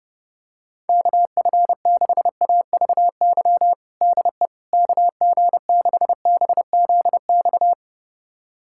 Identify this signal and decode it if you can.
Morse code, heard as a keyed tone, about 30 wpm, 700 Hz. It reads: KF6AVY DE KG6BZX
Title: USA Callsign DE Callsign - 6 Characters - 30wpm